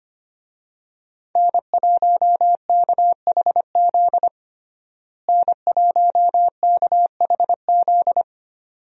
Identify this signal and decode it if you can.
Morse code, heard as a keyed tone, about 25 wpm, 700 Hz. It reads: N1KHZ N1KHZ
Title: USA Call Signs - 5 Characters - Repeated 2x - 25wpm